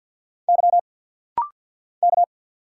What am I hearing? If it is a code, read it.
Morse code, heard as a keyed tone, about 50 wpm, 690 Hz. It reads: Y K
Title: Single Letters - Rapid-Fire 50wpm